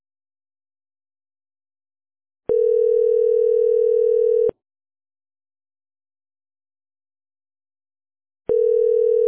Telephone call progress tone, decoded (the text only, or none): ringback tone